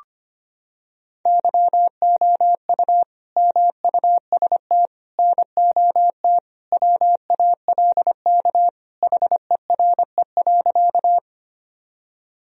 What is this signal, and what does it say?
Morse code, heard as a keyed tone, about 25 wpm, 700 Hz. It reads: YOU MUST NOT WALK HERE.
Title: Sentences from Top 1000 Words - Review Part 1 - 25wpm